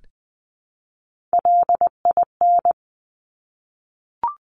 Morse code, heard as a keyed tone, about 20 wpm, 700 Hz. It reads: LIN
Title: Top Three Letter Combinations in Words 20wpm